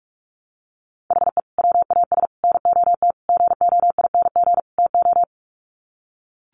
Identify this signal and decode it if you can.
Morse code, heard as a keyed tone, about 44 wpm, 700 Hz. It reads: HE WAS NOT GOING TO
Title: Sets of 5 Words 45wpm